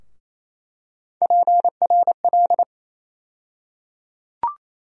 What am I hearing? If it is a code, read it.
Morse code, heard as a keyed tone, about 28 wpm, 700 Hz. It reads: PRL
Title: USA Call Sign Postfixes - 3 Letters 28wpm